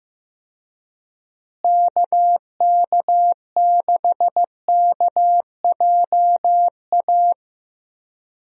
Morse code, heard as a keyed tone, about 15 wpm, 700 Hz. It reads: KK6KJA
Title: USA Call Signs - 6 Characters - Repeated 3x - 15wpm